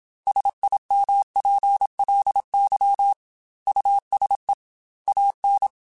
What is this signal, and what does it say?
Morse code, heard as a keyed tone, about 27 wpm, 800 Hz. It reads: SIMPLY USE AN
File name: Technical Subjects / Technical Subjects-mp3s / Sw Pwr Sup @25